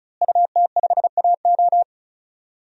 Morse code, heard as a keyed tone, about 34 wpm, 700 Hz. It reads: UT5AO
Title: DXCC-100 Call Signs - 5 Characters - Repeated 3x - 35wpm